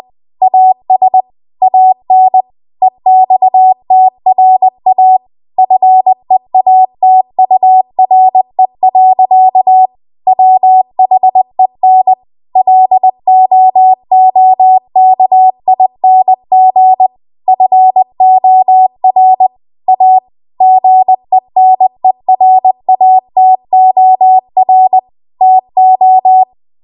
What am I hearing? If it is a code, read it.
Morse code, heard as a keyed tone, about 20 wpm, 750 Hz. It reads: AS AN EXTRA FEATURE. WHEN LOOKING FOR A GENERATOR TO
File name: Archive / 20 WPM / 20WPM